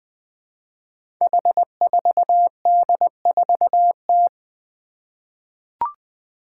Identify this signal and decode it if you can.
Morse code, heard as a keyed tone, about 20 wpm, 700 Hz. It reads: H4D4T
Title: Five Challenging Letter-Number Combinations 20wpm